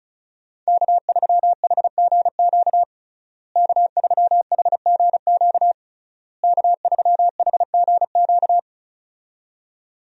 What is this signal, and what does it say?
Morse code, heard as a keyed tone, about 36 wpm, 700 Hz. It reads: K3HGQ K3HGQ K3HGQ
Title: USA Call Signs - 5 Characters - Repeated 3x - 35wpm